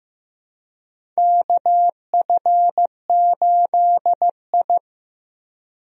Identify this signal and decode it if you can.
Morse code, heard as a keyed tone, about 15 wpm, 700 Hz. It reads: KF8I
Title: USA Call Signs - 4 Characters - Repeated 2x - 15wpm